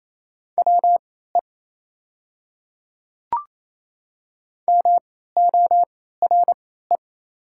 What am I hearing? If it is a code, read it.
Morse code, heard as a keyed tone, about 28 wpm, spaced as for 16 wpm, 700 Hz. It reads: WE MORE
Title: 3x Character Spacing - Top 100 Words - 28wpm